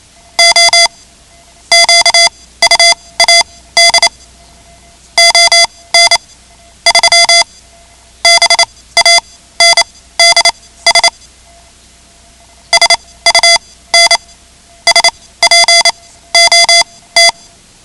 Morse code, heard as a keyed tone, about 28 wpm, spaced as for 17 wpm, 2050 Hz. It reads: O QUAD ON 3 BANDS SUN SPOT
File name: ham radio / CodeQuick / Code Quick 10